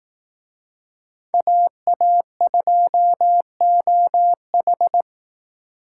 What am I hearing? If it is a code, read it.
Morse code, heard as a keyed tone, about 19 wpm, 700 Hz. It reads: AA2OH
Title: Speed Racing - USA 2x2 Call Signs 15wpm